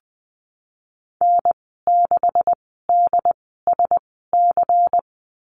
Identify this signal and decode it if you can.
Morse code, heard as a keyed tone, about 20 wpm, spaced as for 15 wpm, 700 Hz. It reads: N6DSC
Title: 147 Lesson 24g - Letter y - Call Signs 20wpm